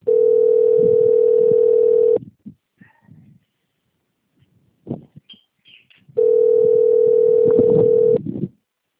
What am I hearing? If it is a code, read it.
Ringback tone (North America), 440 + 480 Hz.
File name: out- - - - - .1461